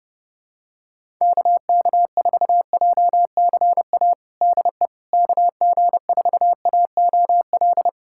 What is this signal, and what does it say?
Morse code, heard as a keyed tone, about 32 wpm, 700 Hz. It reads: KK4JCA DE KG4AOL
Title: USA Callsign DE Callsign - 6 Characters - 30wpm